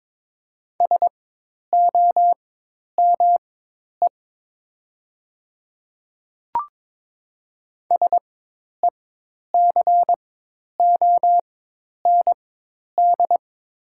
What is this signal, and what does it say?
Morse code, heard as a keyed tone, about 22 wpm, spaced as for 10 wpm, 700 Hz. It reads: SOME SECOND
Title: 4x Character Spacing - Top 500 Words - 22wpm